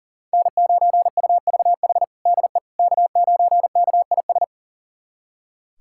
Morse code, heard as a keyed tone, about 40 wpm, 700 Hz. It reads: N9UVH DE K9KIS
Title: USA Callsign DE Callsign - 5 Characters - 40wpm